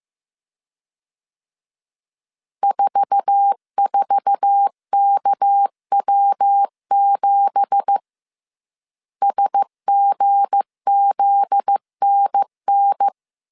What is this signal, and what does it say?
Morse code, heard as a keyed tone, about 15 wpm, 800 Hz. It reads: 44KW7 SGZNN
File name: CW / DET / 1 9 1500C S C 14wpm 3245